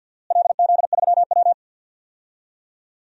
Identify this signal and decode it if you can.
Morse code, heard as a keyed tone, about 48 wpm, 700 Hz. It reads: LZ3W
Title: DXCC-100 Call Signs - 4 Characters - Repeated 3x - 50wpm